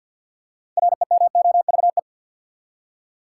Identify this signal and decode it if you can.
Morse code, heard as a keyed tone, about 53 wpm, 700 Hz. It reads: REMOVE
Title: Six Letter Words - 50wpm